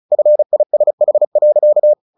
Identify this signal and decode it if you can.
Morse code, heard as a keyed tone, about 34 wpm, 600 Hz. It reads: FISH.
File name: the old man and the sea 2 35wpm